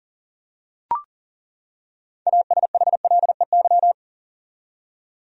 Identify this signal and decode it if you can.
Morse code, heard as a keyed tone, about 40 wpm, 700 Hz. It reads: ASHLEY
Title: USA Common Names - 40wpm